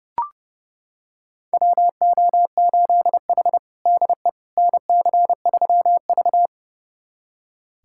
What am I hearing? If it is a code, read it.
Morse code, heard as a keyed tone, about 30 wpm, 700 Hz. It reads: WO8H DE NC3V
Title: USA Callsign DE Callsign - 4 Characters - 30wpm